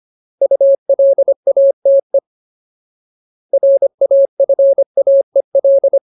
Morse code, heard as a keyed tone, about 25 wpm, 550 Hz. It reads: ULATE RAFAEL
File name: Ultimate Word List / 25WPM Ultimate